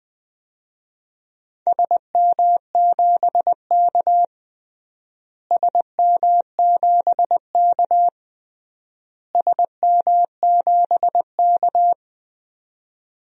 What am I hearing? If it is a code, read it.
Morse code, heard as a keyed tone, about 20 wpm, 700 Hz. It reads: SM7K SM7K SM7K
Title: DXCC-100 Call Signs - 4 Characters - Repeated 3x - 20wpm